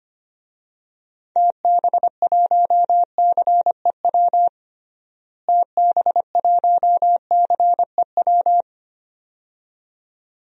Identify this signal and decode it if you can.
Morse code, heard as a keyed tone, about 25 wpm, 700 Hz. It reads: TB1CEW TB1CEW
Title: DXCC-100 Call Signs - 6 Characters - Repeated 2x - 25wpm